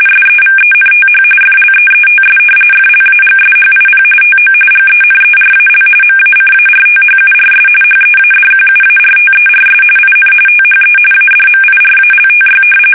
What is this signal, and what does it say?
RTTY, about 45.45 baud. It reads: SEVENTH FLEETS FLAGSHIP TO GENERAL MACARTHURS HEADQUARTERS IN THE ADMIRALTY
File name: fsk850 / battleshipnj